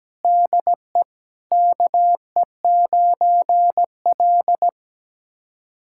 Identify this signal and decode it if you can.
Morse code, heard as a keyed tone, about 17 wpm, 700 Hz. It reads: DE KE9L
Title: USA Callsign DE Callsign - 4 Characters - 17wpm